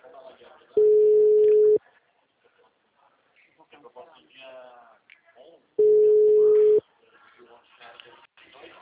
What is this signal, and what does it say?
Ringback tone (Europe, 425 Hz), with speech.